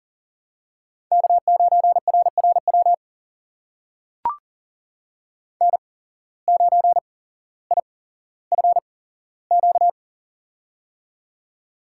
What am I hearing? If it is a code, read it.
Morse code, heard as a keyed tone, about 40 wpm, spaced as for 17 wpm, 700 Hz. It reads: K9RRW N9IFQ
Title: USA Call Signs - 5 Characters - 8x Character Spacing - 40wpm